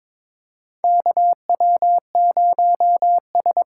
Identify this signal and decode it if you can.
Morse code, heard as a keyed tone, about 22 wpm, 700 Hz. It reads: KW0S
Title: USA Call Signs - 4x Character Spacing - 22wpm